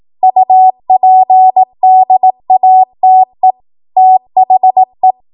Morse code, heard as a keyed tone, about 18 wpm, 750 Hz. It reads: UPDATE THE
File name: Archive / 18 WPM / 18WPM